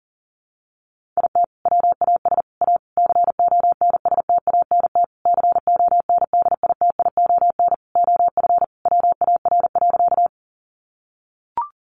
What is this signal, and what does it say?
Morse code, heard as a keyed tone, about 40 wpm, 700 Hz. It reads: IT WAS A CONSTANT CONDITION OF WAR.
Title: Sentences from Top 1500 Words 40wpm